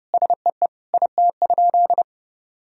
Morse code, heard as a keyed tone, about 32 wpm, 700 Hz. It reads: SEE IT?
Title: Sentences from Top 1000 Words - Review Part 4 - 30wpm